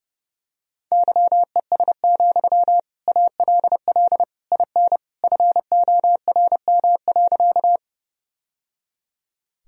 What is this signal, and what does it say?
Morse code, heard as a keyed tone, about 30 wpm, 700 Hz. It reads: YES, ALL IN FORM.
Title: Sentences from Top 1000 Words - Review Part 1 - 30wpm